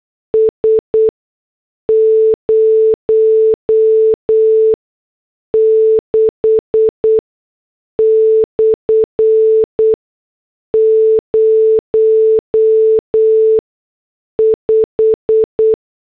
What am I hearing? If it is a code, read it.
Morse code, heard as a keyed tone, about 8 wpm, spaced as for 6 wpm, 440 Hz. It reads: S 06/05